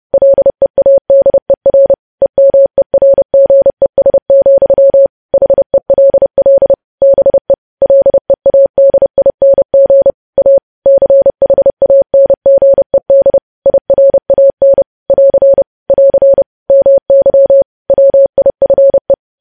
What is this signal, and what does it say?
Morse code, heard as a keyed tone, about 30 wpm, 550 Hz. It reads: LEADER EMERGES, HELL BE LEADING A CHANGED IRAN <AR> <AR> MY WIFE
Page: News Headlines in Morse Code at 30 WPM